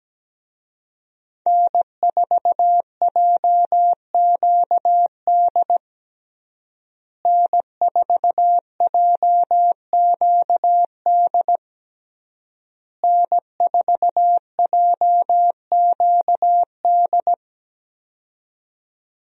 Morse code, heard as a keyed tone, about 17 wpm, 700 Hz. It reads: N4JQD N4JQD N4JQD
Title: USA Call Signs - 5 Characters - Repeated 3x - 17wpm